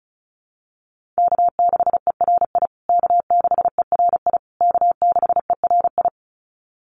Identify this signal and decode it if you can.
Morse code, heard as a keyed tone, about 34 wpm, 700 Hz. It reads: K6ERI K6ERI K6ERI
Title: USA Call Signs Repeated 3x 35wpm